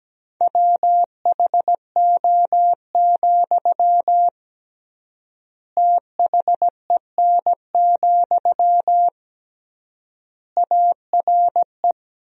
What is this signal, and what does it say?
Morse code, heard as a keyed tone, about 17 wpm, 700 Hz. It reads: WHO, THEN, ARE
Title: 3x Word-Spacing - Sentences from Top 100 Words - 17wpm